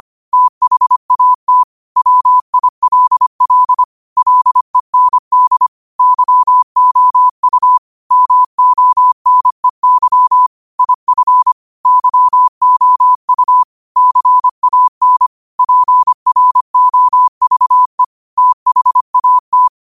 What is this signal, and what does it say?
Morse code, heard as a keyed tone, about 25 wpm, 1000 Hz. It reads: THAT WILL LEND YOU MONEY IF YOU CAN PROVE THAT
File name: qotdfiles / 25wpm